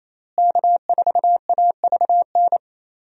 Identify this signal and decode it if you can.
Morse code, heard as a keyed tone, about 28 wpm, 700 Hz. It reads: K4AVN
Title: USA Call Signs - 5 Characters - Repeated 3x - 28wpm